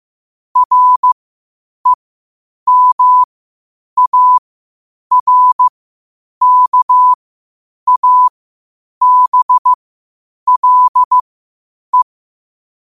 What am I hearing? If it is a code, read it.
Morse code, heard as a keyed tone, about 15 wpm, spaced as for 8 wpm, 1000 Hz. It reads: REMARKABLE
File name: qotdfiles / 8wpm